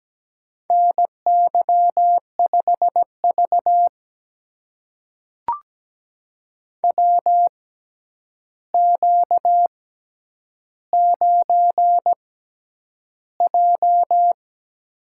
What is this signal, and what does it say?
Morse code, heard as a keyed tone, about 17 wpm, spaced as for 9 wpm, 700 Hz. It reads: NY5V WQ9J
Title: USA Call Signs - 4 Characters - 6x Character Spacing - 17wpm